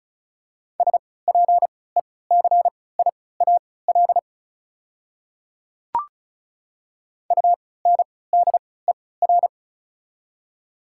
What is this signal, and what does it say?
Morse code, heard as a keyed tone, about 36 wpm, spaced as for 20 wpm, 690 Hz. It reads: SPECIAL UNDER
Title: 3x Character Spacing - Top 500 Words - 35wpm